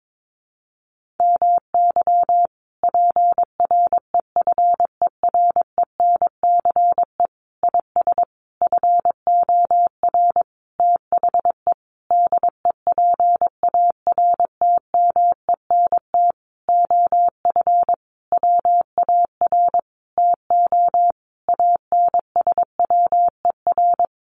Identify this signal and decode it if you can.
Morse code, heard as a keyed tone, about 22 wpm, 700 Hz. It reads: MY PREFERENCE IS FOR THE DEPARTMENT OF WAR TO ANSWER
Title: Sentences from Top 1500 Words 22wpm